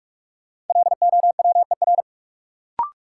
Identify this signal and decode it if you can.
Morse code, heard as a keyed tone, about 44 wpm, 700 Hz. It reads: LOWER
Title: Sentences from Top 700 Words - New Word Drill - 45wpm